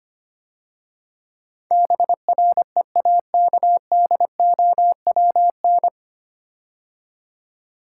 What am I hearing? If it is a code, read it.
Morse code, heard as a keyed tone, about 25 wpm, 700 Hz. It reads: BREAKDOWN
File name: practice / 25wpm / 9 Letter Words - First 3 Letters Given - 25wpm